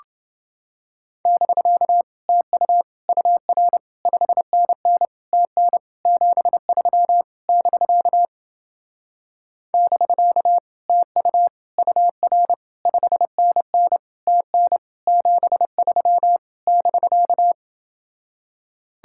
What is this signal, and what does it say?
Morse code, heard as a keyed tone, about 29 wpm, 700 Hz. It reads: <BK> TU UR 5NN TN 73 <BK> <BK> TU UR 5NN TN 73 <BK>